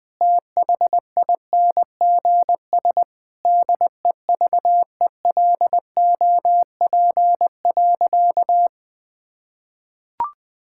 Morse code, heard as a keyed tone, about 19 wpm, 700 Hz. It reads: THINGS DEVELOP.
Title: Sentences with 6-Word Phrases - 20wpm